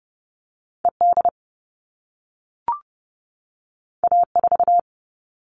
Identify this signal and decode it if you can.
Morse code, heard as a keyed tone, about 30 wpm, 700 Hz. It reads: ED A4
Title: All Cut Numbers - 2-Digit 30wpm